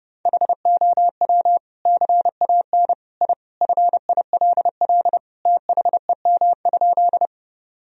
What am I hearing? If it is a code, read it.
Morse code, heard as a keyed tone, about 30 wpm, 700 Hz. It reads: HOW CAN I FILL THEM?
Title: Sentences from Top 1000 Words - Review Part 3 - 30wpm